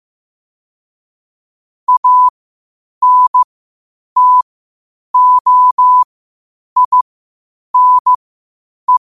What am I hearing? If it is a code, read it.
Morse code, heard as a keyed tone, about 15 wpm, spaced as for 8 wpm, 1000 Hz. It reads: ANTOINE
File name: qotdfiles / 8wpm